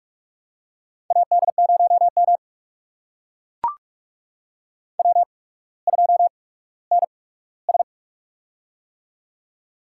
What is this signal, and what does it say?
Morse code, heard as a keyed tone, about 44 wpm, spaced as for 19 wpm, 700 Hz. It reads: AD0M W2NS
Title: USA Call Signs - 8x Character Spacing - 45wpm